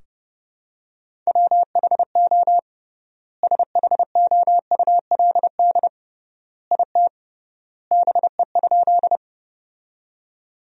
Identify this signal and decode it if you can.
Morse code, heard as a keyed tone, about 30 wpm, 700 Hz. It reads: WHO SHOULD IT BE?
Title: 3x Word-Spacing - Sentences from Top 300 Words - 30wpm